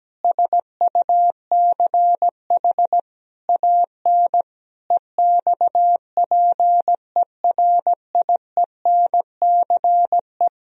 Morse code, heard as a keyed tone, about 17 wpm, 700 Hz. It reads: SUCH AN EXPERIENCE
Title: Sentences from Top 1000 Words - Review Part 3 - 17wpm